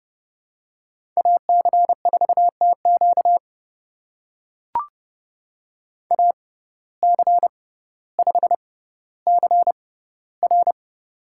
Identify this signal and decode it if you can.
Morse code, heard as a keyed tone, about 30 wpm, spaced as for 15 wpm, 700 Hz. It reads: AC4TQ AC5CR
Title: USA Call Signs - 5 Characters - 6x Character Spacing - 30wpm